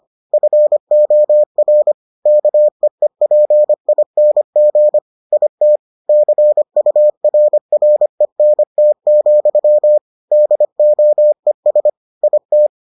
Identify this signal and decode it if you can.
Morse code, heard as a keyed tone, about 25 wpm, 600 Hz. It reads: FOR KEEPING IT CURRENT, DOES IT